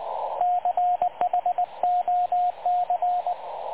Morse code, heard as a keyed tone, about 19 wpm, 720 Hz, with radio noise. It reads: CHOC
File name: dlazaawansowanych / nagraniazpasma / websdr recording 3567.0kHz